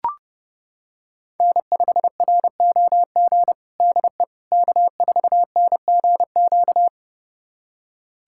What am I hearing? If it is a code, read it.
Morse code, heard as a keyed tone, about 30 wpm, 700 Hz. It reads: N5ROG DE K4NGQ
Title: USA Callsign DE Callsign - 5 Characters - 30wpm